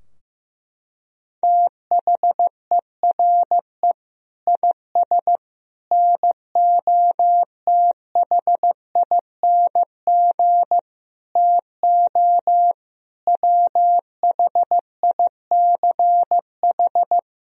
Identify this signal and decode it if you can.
Morse code, heard as a keyed tone, about 16 wpm, 700 Hz. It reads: THERE IS NOTHING TO WHICH
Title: Sentences from Top 1000 Words - Review Part 3 - 15wpm